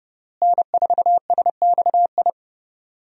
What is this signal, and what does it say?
Morse code, heard as a keyed tone, about 29 wpm, 700 Hz. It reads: N4SXI
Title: USA Call Signs - 5 Characters - Repeated 3x - 30wpm